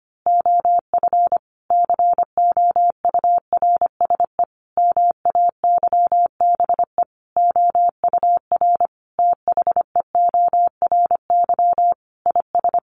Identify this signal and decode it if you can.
Morse code, heard as a keyed tone, about 25 wpm, 700 Hz. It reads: OF COURSE MAYBE OUR THEORY IS
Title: A Sound Of Thunder 25wpm